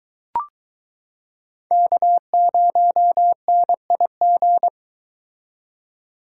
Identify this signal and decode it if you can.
Morse code, heard as a keyed tone, about 21 wpm, 700 Hz. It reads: K0NIG